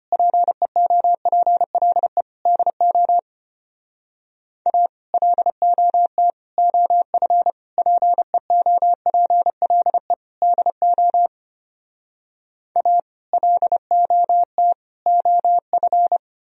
Morse code, heard as a keyed tone, about 33 wpm, 700 Hz. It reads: PEOPLE DO A LOT OF PEOPLE DO A LOT OF